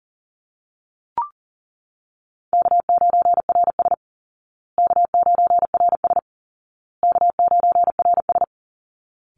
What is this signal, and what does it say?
Morse code, heard as a keyed tone, about 40 wpm, 700 Hz. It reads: K9RS K9RS K9RS
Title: Easy Call Signs Repeated 3x - 40wpm